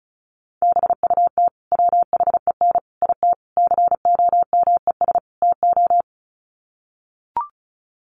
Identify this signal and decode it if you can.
Morse code, heard as a keyed tone, about 34 wpm, 700 Hz. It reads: BUT WHEN IT COMES TO
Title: Sets of 5 Words 35wpm